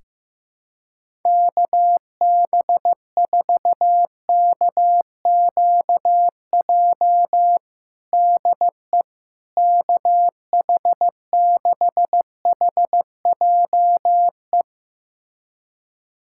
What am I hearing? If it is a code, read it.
Morse code, heard as a keyed tone, about 15 wpm, 700 Hz. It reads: KB4KQJ DE KH6HJE
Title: USA Callsign DE Callsign 15wpm